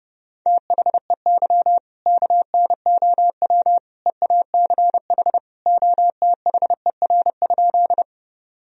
Morse code, heard as a keyed tone, about 32 wpm, 700 Hz. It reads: THEY KNOW EACH OTHER?
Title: Sentences from Top 1000 Words - Review Part 1 - 30wpm